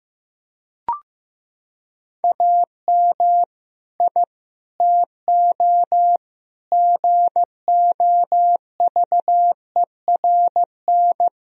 Morse code, heard as a keyed tone, about 16 wpm, 700 Hz. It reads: AM I TO GOVERN
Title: Sentences from Top 1000 Words - Review Part 2 - 15wpm